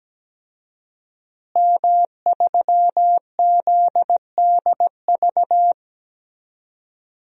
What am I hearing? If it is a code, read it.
Morse code, heard as a keyed tone, about 17 wpm, 700 Hz. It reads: M3ZDV
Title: DXCC-100 Call Signs - 5 Characters - Repeated 3x - 17wpm